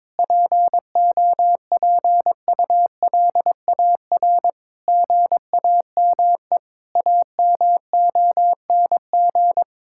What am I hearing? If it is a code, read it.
Morse code, heard as a keyed tone, about 22 wpm, 700 Hz. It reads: POPULAR GAME AMONG
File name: Difficult Sentences with Hints - 22wpm